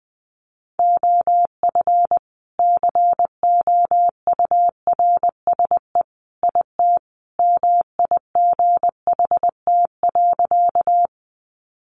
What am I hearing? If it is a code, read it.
Morse code, heard as a keyed tone, about 20 wpm, 700 Hz. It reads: OF COURSE IT MIGHT.
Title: Sentences from Top 500 Words 20wpm